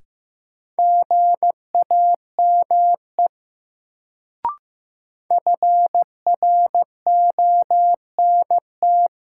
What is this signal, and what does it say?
Morse code, heard as a keyed tone, about 15 wpm, 700 Hz. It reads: GAME FRONT
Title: Sentences from Top 500 Words - New Word Drill - 15wpm